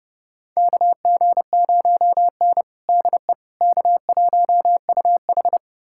Morse code, heard as a keyed tone, about 30 wpm, 700 Hz. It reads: KG0N DE K1UH
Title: USA Callsign DE Callsign - 4 Characters - 30wpm